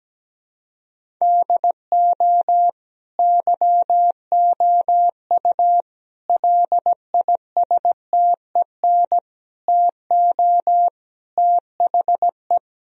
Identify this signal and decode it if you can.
Morse code, heard as a keyed tone, about 17 wpm, 700 Hz. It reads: DO YOU LISTEN TO THE
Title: Sentences from Top 1000 Words - Review Part 3 - 17wpm